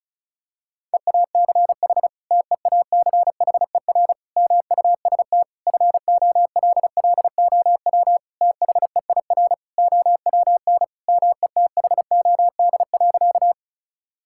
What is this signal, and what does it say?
Morse code, heard as a keyed tone, about 36 wpm, 700 Hz. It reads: EACH TEACHER MUST FOLLOW THEIR OWN METHOD.
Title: Sentences from Top 1000 Words - Review Part 6 - 35wpm